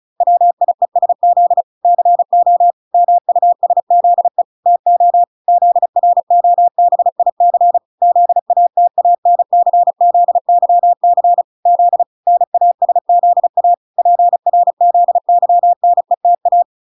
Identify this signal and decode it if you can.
Morse code, heard as a keyed tone, about 34 wpm, 700 Hz. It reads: WIESZ CO MUSZE TO ZROBIC ZATANCZYC Z NASZA PRZYNETA